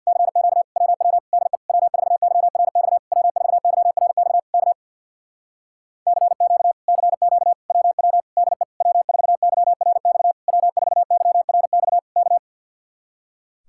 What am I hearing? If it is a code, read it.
Morse code, heard as a keyed tone, about 67 wpm, 700 Hz. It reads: CQ WW DE W4YRX W4YRX K CQ CQ WW DE W4YRX W4YRX K